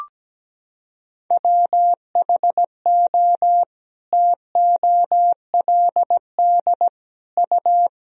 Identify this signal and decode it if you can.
Morse code, heard as a keyed tone, about 17 wpm, 700 Hz. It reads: WHO TOLD U
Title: Sentences from Top 400 Words and Abbreviations 17wpm